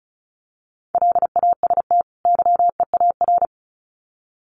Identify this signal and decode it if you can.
Morse code, heard as a keyed tone, about 34 wpm, 700 Hz. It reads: LAST YEAR
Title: Sets of 2 Words 35wpm